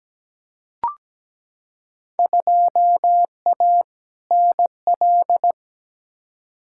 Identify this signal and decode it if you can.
Morse code, heard as a keyed tone, about 17 wpm, 700 Hz. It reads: 2A NL
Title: Field Day Exchanges Encore - 17wpm